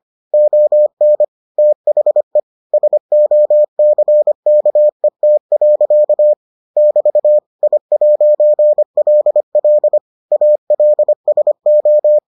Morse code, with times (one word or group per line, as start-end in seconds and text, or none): 0.3-1.2 ON
1.6-2.4 THE
2.7-6.3 SOCKET.
6.8-7.4 <BT>
7.6-10.0 I'LL
10.3-12.2 ALSO